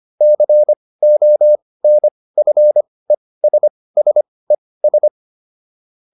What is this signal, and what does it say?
Morse code, heard as a keyed tone, about 25 wpm, spaced as for 18 wpm, 600 Hz. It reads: CONFESSES